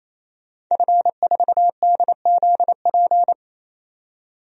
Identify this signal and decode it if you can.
Morse code, heard as a keyed tone, about 28 wpm, 700 Hz. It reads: F4DZP
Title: DXCC-100 Call Signs - 5 Characters - 28wpm